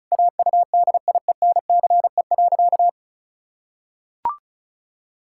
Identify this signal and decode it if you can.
Morse code, heard as a keyed tone, about 36 wpm, 700 Hz. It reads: AUDIENCE.
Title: Pangram Challenge v2 - 35wpm